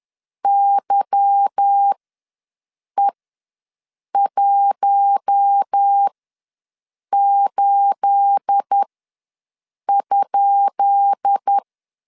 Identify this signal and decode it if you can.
Morse code, heard as a keyed tone, about 10 wpm, spaced as for 6 wpm, 800 Hz. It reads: YE18?
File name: LEZIONI / 10 SIGNS / 10B INT ws 10wpm 4134